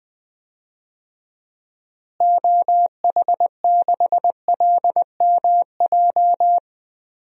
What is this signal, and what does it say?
Morse code, heard as a keyed tone, about 20 wpm, 700 Hz. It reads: OH6LMJ
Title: DXCC-100 Call Signs - 6 Characters - Repeated 3x - 20wpm